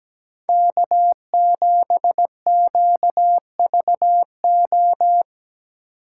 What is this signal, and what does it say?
Morse code, heard as a keyed tone, about 18 wpm, 700 Hz. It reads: K7QVO
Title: Speed Racing - USA 1x3 Call Signs 17wpm